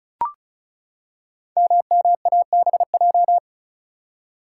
Morse code, heard as a keyed tone, about 36 wpm, 700 Hz. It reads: MMABJ
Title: Code Groups - 5 Letter:Numbers - 35wpm